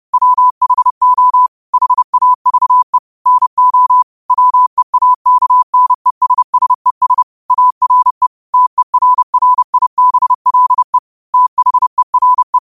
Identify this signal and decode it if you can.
Morse code, heard as a keyed tone, about 30 wpm, 1000 Hz. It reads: WHO HAVE NO WEAKNESSES ARE TERRIBLE THERE
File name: qotdfiles / 30wpm